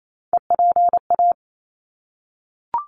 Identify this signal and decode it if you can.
Morse code, heard as a keyed tone, about 28 wpm, 700 Hz. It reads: EPA
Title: ARRL Sections 28wpm